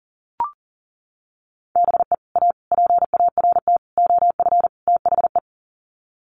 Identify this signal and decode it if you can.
Morse code, heard as a keyed tone, about 40 wpm, 700 Hz. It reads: BE A PART OF THE
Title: Sets of 5 Words Encore 40wpm